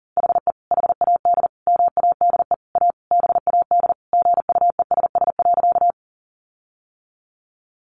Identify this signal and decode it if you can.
Morse code, heard as a keyed tone, about 37 wpm, 700 Hz. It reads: HE HAD MADE A BAD GUESS.
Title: Sentences from Top 1500 Words 40wpm